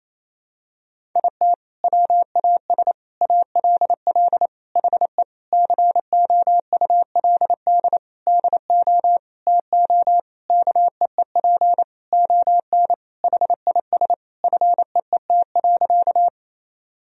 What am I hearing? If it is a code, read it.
Morse code, heard as a keyed tone, about 28 wpm, 700 Hz. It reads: IT WAS ALL HE COULD DO TO KEEP ON HIS FEET.
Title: Sentences from Top 1000 Words - Review Part 1 - 28wpm